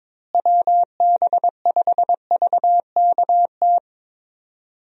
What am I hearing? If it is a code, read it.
Morse code, heard as a keyed tone, about 22 wpm, 700 Hz. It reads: WB5VKT
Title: USA Call Signs - 6 Characters - Repeated 3x - 22wpm